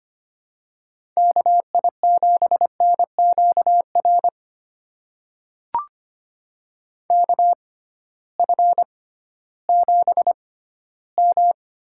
Morse code, heard as a keyed tone, about 25 wpm, spaced as for 14 wpm, 700 Hz. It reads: KI7NQR KF7M
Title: USA Call Signs - 6 Characters - 6x Character Spacing - 25wpm